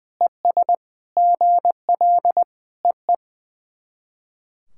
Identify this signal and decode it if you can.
Morse code, heard as a keyed tone, about 21 wpm, 690 Hz. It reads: ES GL EE
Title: SOTA QSO Practice 20wpm